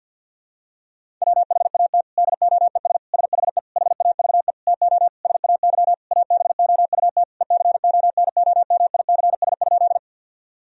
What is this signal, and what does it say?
Morse code, heard as a keyed tone, about 50 wpm, 700 Hz. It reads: WHAT DOES SHE HAVE TO SAY ABOUT ECONOMICS?
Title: Sentences from Top 1000 Words - Review Part 3 - 50wpm